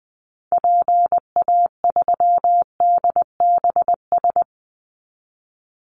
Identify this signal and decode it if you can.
Morse code, heard as a keyed tone, about 20 wpm, 700 Hz. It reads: PA3DBS
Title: Call Signs - 20wpm